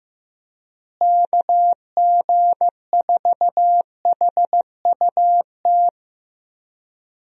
Morse code, heard as a keyed tone, about 15 wpm, 700 Hz. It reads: KG4HUT
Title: USA Call Signs - 6 Characters - Repeated 3x - 15wpm